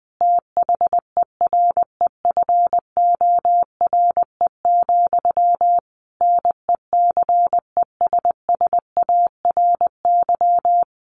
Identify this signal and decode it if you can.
Morse code, heard as a keyed tone, about 20 wpm, 700 Hz. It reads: THEREFORE, NECESSARY
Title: Sentences from Top 1500 Words 20wpm